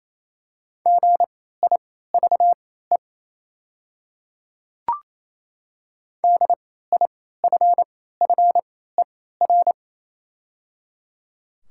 Morse code, heard as a keyed tone, about 28 wpm, spaced as for 16 wpm, 690 Hz. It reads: GIVE DIFFER
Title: 3x Character Spacing - Top 500 Words - 28wpm